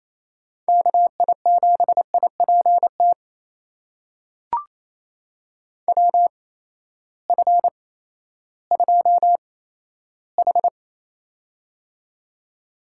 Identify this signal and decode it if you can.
Morse code, heard as a keyed tone, about 28 wpm, spaced as for 14 wpm, 700 Hz. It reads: KI7IPT WF2H
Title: USA Call Signs - 8x Character Spacing - 28wpm